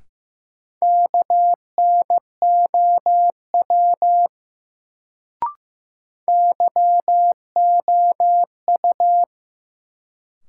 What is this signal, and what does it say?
Morse code, heard as a keyed tone, about 15 wpm, 700 Hz. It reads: KNOW YOU
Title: Sets of 2 Words - New Word Drill - 15wpm